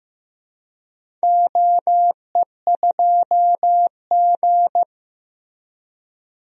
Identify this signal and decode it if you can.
Morse code, heard as a keyed tone, about 15 wpm, 700 Hz. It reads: OE2G
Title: DXCC-100 Call Signs - 4 Characters - Repeated 3x - 15wpm